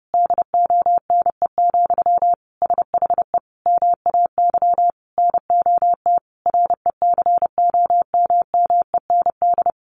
Morse code, heard as a keyed tone, about 30 wpm, 700 Hz. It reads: DONE, SHE MAY NOT RECOMMEND
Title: Sentences from Top 1200 Words 30wpm